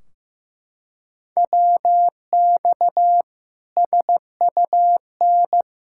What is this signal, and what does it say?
Morse code, heard as a keyed tone, about 15 wpm, 700 Hz. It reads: WX SUN
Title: Practice QSO - Exchange 2 - Info - 15wpm